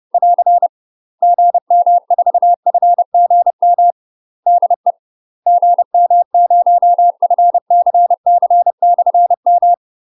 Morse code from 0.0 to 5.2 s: <AR> GM4FGM DE